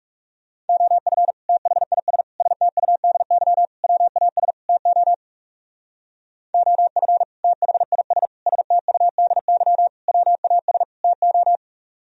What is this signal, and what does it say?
Morse code, heard as a keyed tone, about 35 wpm, 700 Hz. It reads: OF THIS STUDY WAS TO OF THIS STUDY WAS TO
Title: Speed-Racing - Sets of 5 Words 30wpm